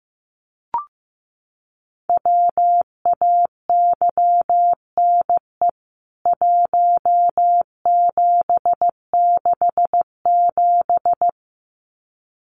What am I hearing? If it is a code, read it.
Morse code, heard as a keyed tone, about 15 wpm, 700 Hz. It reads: WAYNE 1767
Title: CWT Exchange 15wpm